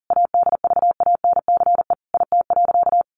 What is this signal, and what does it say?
Morse code, heard as a keyed tone, about 40 wpm, 700 Hz. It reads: ADVANCE IT.
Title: Sentences from Top 1400 Words 40wpm